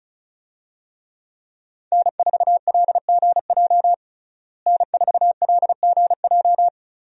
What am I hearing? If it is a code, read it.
Morse code, heard as a keyed tone, about 34 wpm, 700 Hz. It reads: N4LGJ N4LGJ
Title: USA Call Signs - 5 Characters - Repeated 3x - 35wpm